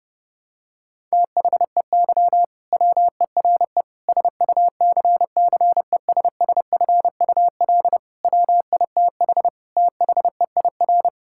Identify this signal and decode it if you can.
Morse code, heard as a keyed tone, about 30 wpm, 700 Hz. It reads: THEY WERE SUCCESSFUL WITH THEIR
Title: Sentences from Top 1000 Words - Review Part 6 - 30wpm